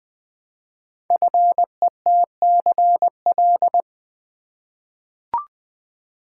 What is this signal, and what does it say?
Morse code, heard as a keyed tone, about 20 wpm, 700 Hz. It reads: FETCL
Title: Code Groups - 5 Letters - 20wpm